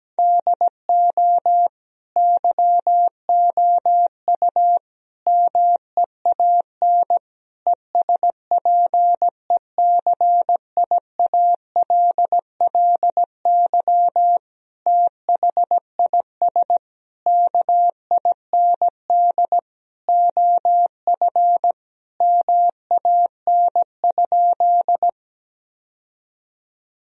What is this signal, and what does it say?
Morse code, heard as a keyed tone, about 17 wpm, 700 Hz. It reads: DO YOU MEAN ESPECIALLY THIS KIND OF MAN?
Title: Sentences from Top 1000 Words - Review Part 4 - 17wpm